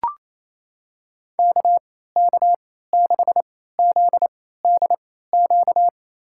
Morse code, heard as a keyed tone, about 28 wpm, spaced as for 16 wpm, 700 Hz. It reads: KK6ZDQ
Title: USA Call Signs - 6 Characters - 3x Character Spacing - 28wpm